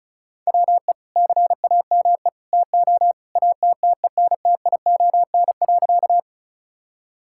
Morse code, heard as a keyed tone, about 36 wpm, 690 Hz. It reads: WE CAME TO ATTENTION.
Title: Sentences from Top 1000 Words - Review Part 6 - 35wpm